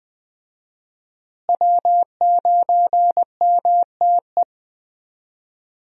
Morse code, heard as a keyed tone, about 20 wpm, 700 Hz. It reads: W9MTE
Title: USA Call Signs - 5 Characters - Repeated 2x - 20wpm